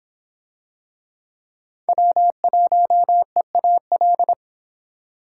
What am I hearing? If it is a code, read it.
Morse code, heard as a keyed tone, about 26 wpm, 700 Hz. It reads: W1EAL